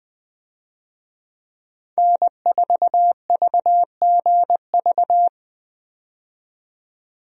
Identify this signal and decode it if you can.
Morse code, heard as a keyed tone, about 20 wpm, 700 Hz. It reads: N4VGV
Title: USA Call Signs - 5 Characters - 20wpm